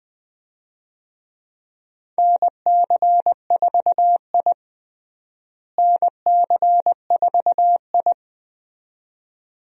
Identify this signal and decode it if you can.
Morse code, heard as a keyed tone, about 21 wpm, 700 Hz. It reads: NC4I NC4I
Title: USA Call Signs - 4 Characters - Repeated 2x - 20wpm